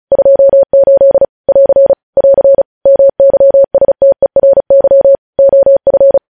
Morse code, heard as a keyed tone, about 36 wpm, 550 Hz. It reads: 28 <AR> <AR> MYSTERY OF